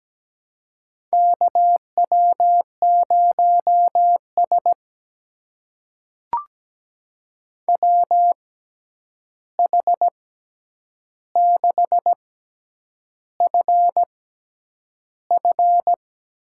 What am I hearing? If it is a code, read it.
Morse code, heard as a keyed tone, about 17 wpm, spaced as for 8 wpm, 700 Hz. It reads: KW0S WH6FF
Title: USA Call Signs - 6x Character Spacing - 17wpm